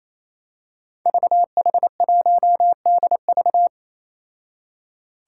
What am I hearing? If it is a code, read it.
Morse code, heard as a keyed tone, about 27 wpm, 700 Hz. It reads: VH1DV
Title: Five Challenging Letter-Number Combinations 28wpm